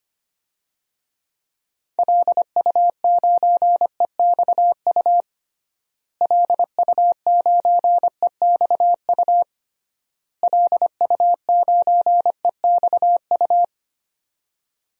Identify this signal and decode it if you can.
Morse code, heard as a keyed tone, about 25 wpm, 700 Hz. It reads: LU9EXU LU9EXU LU9EXU
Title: DXCC-100 Call Signs - 6 Characters - Repeated 3x - 25wpm